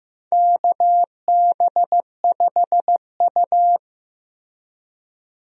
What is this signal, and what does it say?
Morse code, heard as a keyed tone, about 15 wpm, 700 Hz. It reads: KB5U
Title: USA Call Signs - 4 Characters - Repeated 3x - 15wpm